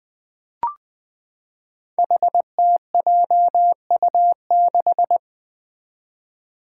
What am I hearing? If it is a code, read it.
Morse code, heard as a keyed tone, about 20 wpm, 700 Hz. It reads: HTJU6
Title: Five Challenging Letter-Number Combinations 20wpm